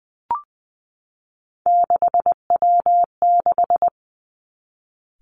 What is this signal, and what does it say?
Morse code, heard as a keyed tone, about 20 wpm, 700 Hz. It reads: 6W6
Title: Three Challenging Letter-Number Combinations 20wpm